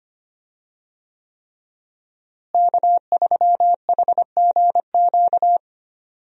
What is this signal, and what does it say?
Morse code, heard as a keyed tone, about 25 wpm, 700 Hz. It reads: K3HGQ
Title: USA Call Signs - 5 Characters - Repeated 3x - 25wpm